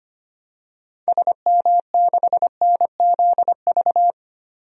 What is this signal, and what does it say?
Morse code, heard as a keyed tone, about 25 wpm, 700 Hz. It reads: SM6NZV
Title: DXCC-100 Call Signs - 6 Characters - 25wpm